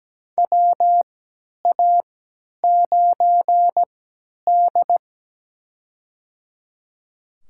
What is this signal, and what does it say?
Morse code, heard as a keyed tone, about 17 wpm, spaced as for 10 wpm, 700 Hz. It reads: WA9D
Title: USA Call Signs - 4 Characters - 3x Character Spacing - 17wpm